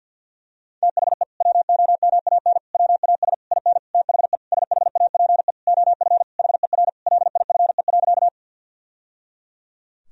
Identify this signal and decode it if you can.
Morse code, heard as a keyed tone, about 50 wpm, 700 Hz. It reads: THE WOMAN WAS IN THE SHAPE OF HER LIFE.
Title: Sentences from Top 1000 Words - Review Part 3 - 50wpm